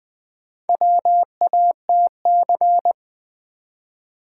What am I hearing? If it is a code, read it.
Morse code, heard as a keyed tone, about 20 wpm, 700 Hz. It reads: WATC
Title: Build A Word - 1 Letter At A Time - 5-Letter Words - 20wpm